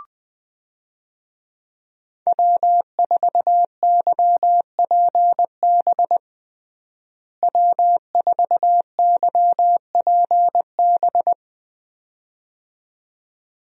Morse code, heard as a keyed tone, about 20 wpm, 700 Hz. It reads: W4YPB W4YPB
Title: USA Call Signs - 5 Characters - Repeated 2x - 20wpm